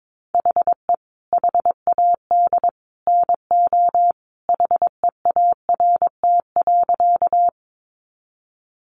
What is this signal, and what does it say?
Morse code, heard as a keyed tone, about 22 wpm, 700 Hz. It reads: HE HAD NO HEART.
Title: Sentences from Top 1200 Words 22wpm